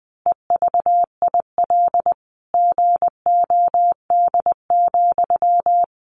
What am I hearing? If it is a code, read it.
Morse code, heard as a keyed tone, about 20 wpm, 700 Hz. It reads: EVIL GOD,
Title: A Sound Of Thunder 20wpm